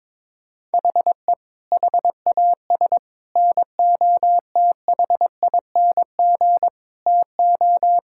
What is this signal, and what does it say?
Morse code, heard as a keyed tone, about 21 wpm, 700 Hz. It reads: HE HAS NOTHING TO
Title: Sentences from Top 1000 Words - Review Part 5 - 22wpm